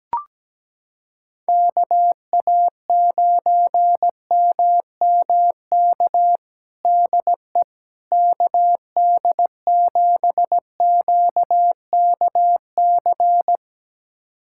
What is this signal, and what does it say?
Morse code, heard as a keyed tone, about 17 wpm, 700 Hz. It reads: KA9MMK DE KD7QKC
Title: USA Callsign DE Callsign 17wpm